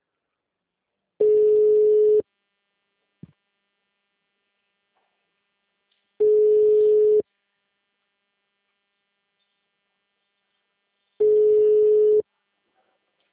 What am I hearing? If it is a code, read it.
Ringback tone (Europe, 425 Hz).